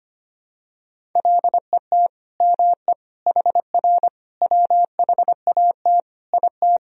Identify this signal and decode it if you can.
Morse code, heard as a keyed tone, about 25 wpm, 700 Hz. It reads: LET ME HR WHAT IT
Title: Sentences from Top 300 Words and Abbreviations 25wpm